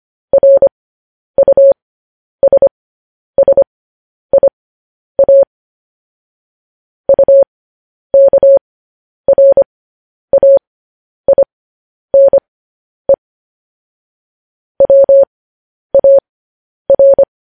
Morse code, heard as a keyed tone, about 25 wpm, spaced as for 10 wpm, 550 Hz. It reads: RUSSIA UKRAINE WAR
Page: news headlines in morse code at 10 wpm